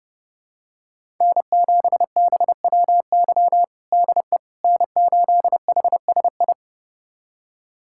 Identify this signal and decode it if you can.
Morse code, heard as a keyed tone, about 30 wpm, 700 Hz. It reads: N7BWY DE N8HSI
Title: USA Callsign DE Callsign - 5 Characters - 30wpm